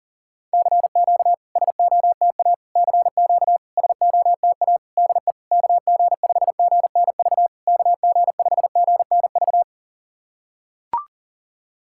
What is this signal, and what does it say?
Morse code, heard as a keyed tone, about 40 wpm, 700 Hz. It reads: CQ SOTA CQ SOTA DE KG5GNV KG5GNV
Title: SOTA QSO Practice 40wpm